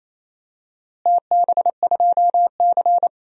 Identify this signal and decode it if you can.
Morse code, heard as a keyed tone, about 28 wpm, 700 Hz. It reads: TB2C
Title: DXCC-100 Call Signs - 4 Characters - Repeated 2x - 28wpm